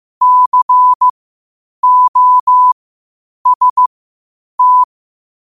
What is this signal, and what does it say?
Morse code, heard as a keyed tone, about 15 wpm, spaced as for 8 wpm, 1000 Hz. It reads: COST
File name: qotdfiles / 8wpm